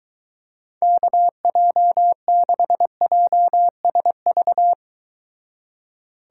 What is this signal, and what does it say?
Morse code, heard as a keyed tone, about 23 wpm, 700 Hz. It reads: KJ6JSV